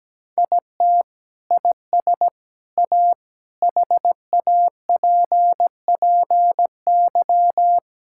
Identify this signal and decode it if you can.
Morse code, heard as a keyed tone, about 17 wpm, 700 Hz. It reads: IT IS A HAPPY
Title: Sentences from Top 1000 Words - Review Part 6 - 17wpm